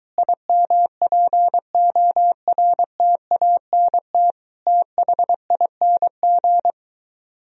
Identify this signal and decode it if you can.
Morse code, heard as a keyed tone, about 23 wpm, 700 Hz. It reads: IMPORTANT THING